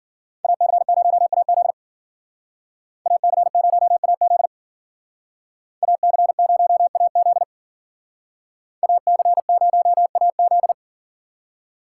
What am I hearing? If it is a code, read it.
Morse code, heard as a keyed tone, about 50 wpm, 700 Hz. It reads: AC0AZ AC0AZ AC0AZ AC0AZ